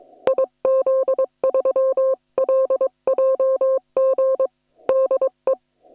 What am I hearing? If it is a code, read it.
Morse code, heard as a keyed tone, about 22 wpm, 530 Hz. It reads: IZ3LJG DE